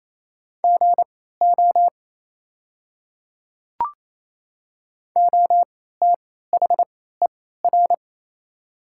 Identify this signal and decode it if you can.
Morse code, heard as a keyed tone, about 28 wpm, spaced as for 16 wpm, 700 Hz. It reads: GO OTHER
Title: 3x Character Spacing - Top 100 Words - 28wpm